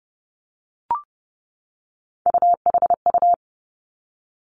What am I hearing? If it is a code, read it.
Morse code, heard as a keyed tone, about 30 wpm, 700 Hz. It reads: UHU
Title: Three Challenging Letter-Number Combinations 30wpm